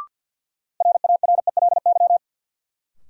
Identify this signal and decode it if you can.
Morse code, heard as a keyed tone, about 53 wpm, 700 Hz. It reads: RARELY
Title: Sentences from Top 1400 Words - New Word Drill Alt - 50wpm